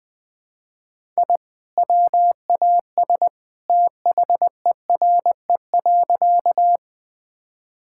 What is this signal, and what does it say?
Morse code, heard as a keyed tone, about 20 wpm, 700 Hz. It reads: I WAS THERE.
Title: Sentences with 2-Word Phrases - 20wpm